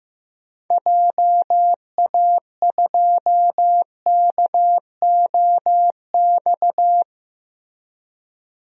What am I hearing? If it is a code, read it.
Morse code, heard as a keyed tone, about 15 wpm, 700 Hz. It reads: JA2KOX
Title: DXCC-100 Call Signs - 6 Characters - 15wpm